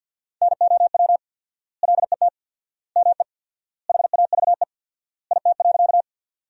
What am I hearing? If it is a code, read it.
Morse code, heard as a keyed tone, about 52 wpm, 700 Hz. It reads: NOW LET ME HAVE IT.
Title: 4x Word-Spacing - Sentences from Top 300 Words - 50wpm